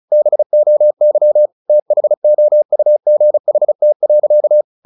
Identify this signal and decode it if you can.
Morse code, heard as a keyed tone, about 34 wpm, 600 Hz. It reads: BOY THOUGHT.
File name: the old man and the sea / the old man and the sea 1 35wpm